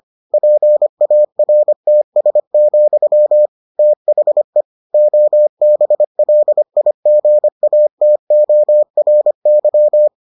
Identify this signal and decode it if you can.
Morse code, heard as a keyed tone, about 25 wpm, 600 Hz. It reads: PARTS, THE OBLIGATORY